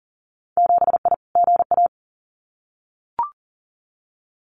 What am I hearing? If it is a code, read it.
Morse code, heard as a keyed tone, about 39 wpm, 700 Hz. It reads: 7I GA
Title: Winter Field Day Exchanges Encore 40wpm